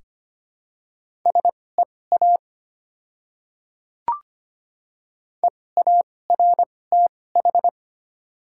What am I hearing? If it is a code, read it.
Morse code, heard as a keyed tone, about 25 wpm, spaced as for 18 wpm, 700 Hz. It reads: SEA EARTH
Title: 2x Character Spacing - Top 300 Words - 25wpm